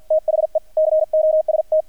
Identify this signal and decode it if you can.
Morse code, heard as a keyed tone, about 50 wpm, 630 Hz, with radio noise. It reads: THE COST
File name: archived / CRSnet